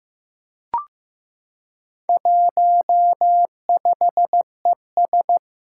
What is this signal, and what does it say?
Morse code, heard as a keyed tone, about 15 wpm, 700 Hz. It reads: 15ES
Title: Four Challenging Letter-Number Combinations 15wpm